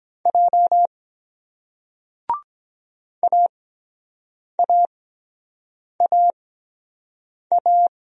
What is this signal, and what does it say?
Morse code, heard as a keyed tone, about 22 wpm, spaced as for 7 wpm, 700 Hz. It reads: J AAAA